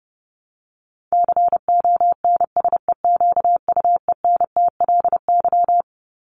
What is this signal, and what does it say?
Morse code, heard as a keyed tone, about 28 wpm, 700 Hz. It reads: CONSEQUENTLY
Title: LY Suffix - Top 100 Words with Base Words 30wpm